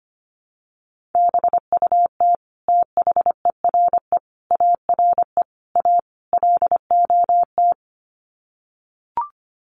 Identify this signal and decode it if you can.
Morse code, heard as a keyed tone, about 25 wpm, 700 Hz. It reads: BUT THERE ARE A LOT
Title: Sets of 5 Words Encore2 25wpm